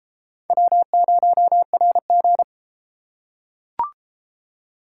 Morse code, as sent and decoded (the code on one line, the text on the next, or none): .-- ----- .-. --.
W0RG